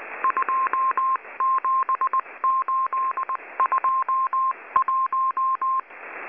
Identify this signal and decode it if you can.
Morse code, heard as a keyed tone, about 19 wpm, 1060 Hz. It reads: 27821
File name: HM02- -0540utc